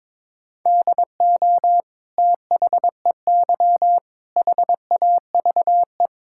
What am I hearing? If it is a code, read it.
Morse code, heard as a keyed tone, about 22 wpm, 700 Hz. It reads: DO THEY HAVE
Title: Sentences from Top 1000 Words - Review Part 5 - 22wpm